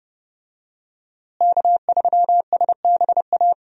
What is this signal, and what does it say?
Morse code, heard as a keyed tone, about 29 wpm, 700 Hz. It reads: K3SBA
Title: USA Call Signs - 5 Characters - Repeated 2x - 30wpm